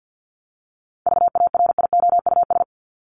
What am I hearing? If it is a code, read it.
Morse code, heard as a keyed tone, about 50 wpm, 700 Hz. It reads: VARIOUS
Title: Top 501-750 Words 50wpm